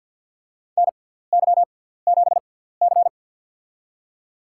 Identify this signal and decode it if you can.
Morse code, heard as a keyed tone, about 50 wpm, spaced as for 17 wpm, 700 Hz. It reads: NY7C
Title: USA Call Signs - 4 Characters - 6x Character Spacing - 50wpm